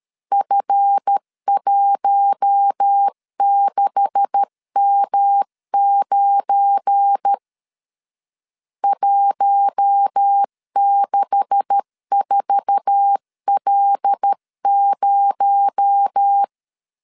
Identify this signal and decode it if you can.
Morse code, heard as a keyed tone, about 13 wpm, 800 Hz. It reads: F16M9 164L0
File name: LEZIONI / 9 286 12wpm 2305